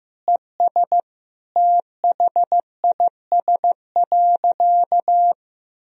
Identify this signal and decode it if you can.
Morse code, heard as a keyed tone, about 15 wpm, 700 Hz. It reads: ES THIS.
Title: Sentences from Top 200 Words and Abbreviations 15wpm